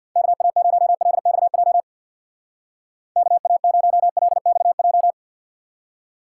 Morse code, heard as a keyed tone, about 61 wpm, 700 Hz. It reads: KA0LXJ KA0LXJ
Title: Speed Racing - USA 2x3 Call Signs 50wpm